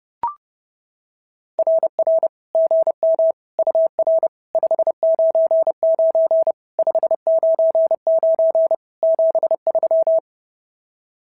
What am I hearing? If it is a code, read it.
Morse code, heard as a keyed tone, about 30 wpm, 650 Hz. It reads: RR GM UR 599 599 73
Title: SOTA QSO Practice 30wpm